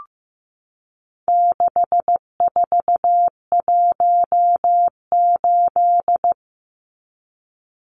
Morse code, heard as a keyed tone, about 15 wpm, 700 Hz. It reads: 6418
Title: Numbers - 4-Digit 15wpm